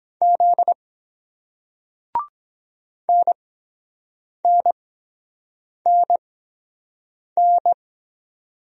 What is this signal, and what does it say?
Morse code, heard as a keyed tone, about 22 wpm, spaced as for 7 wpm, 700 Hz. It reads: Z NNNN